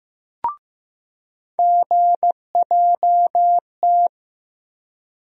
Morse code, heard as a keyed tone, about 15 wpm, 700 Hz. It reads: GJT
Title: USA Call Sign Postfixes - 3 Letters 15wpm